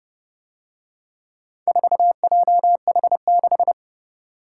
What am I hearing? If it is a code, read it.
Morse code, heard as a keyed tone, about 30 wpm, 700 Hz. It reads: 4JH6